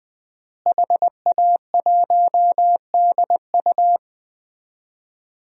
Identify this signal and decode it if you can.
Morse code, heard as a keyed tone, about 20 wpm, 700 Hz. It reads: HA1DU
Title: DXCC-100 Call Signs - 5 Characters - Repeated 2x - 20wpm